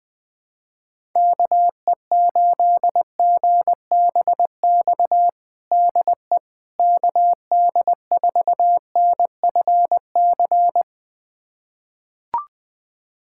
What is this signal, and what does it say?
Morse code, heard as a keyed tone, about 21 wpm, 700 Hz. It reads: KE8GBX DE KD4NFC
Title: USA Callsign DE Callsign - 6 Characters - 20wpm